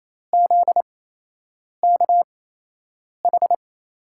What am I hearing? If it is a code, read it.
Morse code, heard as a keyed tone, about 28 wpm, spaced as for 8 wpm, 700 Hz. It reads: ZKH
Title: USA Call Signs - 5 Characters - 8x Character Spacing - 28wpm